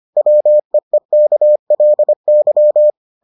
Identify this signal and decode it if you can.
Morse code, heard as a keyed tone, about 25 wpm, 600 Hz. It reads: WEEKLY